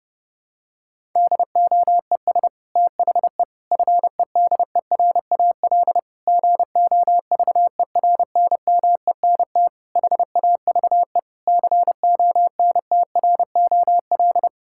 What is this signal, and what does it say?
Morse code, heard as a keyed tone, about 30 wpm, 700 Hz. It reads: DOES THE FEDERAL GOVERNMENT HAVE CONTROL
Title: Sentences from Top 1000 Words - Review Part 5 - 30wpm